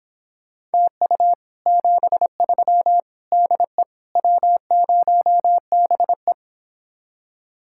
Morse code, heard as a keyed tone, about 29 wpm, 700 Hz. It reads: TU 73 DE W0BE